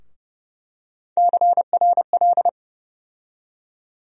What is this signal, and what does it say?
Morse code, heard as a keyed tone, about 30 wpm, 700 Hz. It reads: CRL
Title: USA Call Sign Postfixes - 3 Letters 30wpm